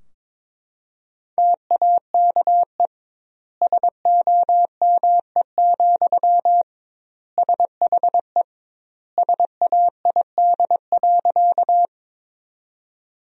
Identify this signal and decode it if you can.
Morse code, heard as a keyed tone, about 23 wpm, 700 Hz. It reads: TAKE SOME, SHE SAID.
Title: 2x Word-Spacing - Sentences from Top 200 Words - 22wpm